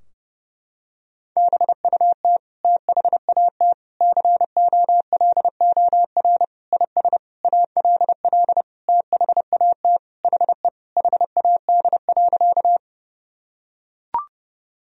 Morse code, heard as a keyed tone, about 30 wpm, 700 Hz. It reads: BUT THAT COLOR IS ALL THAT HE HAD.
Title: Sentences from Top 1000 Words - Review Part 2 - 30wpm